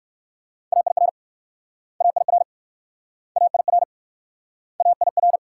Morse code, heard as a keyed tone, about 55 wpm, 690 Hz. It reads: AIR AIR AIR AIR
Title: Speed-Racing Multi-letter Phonemes - 45wpm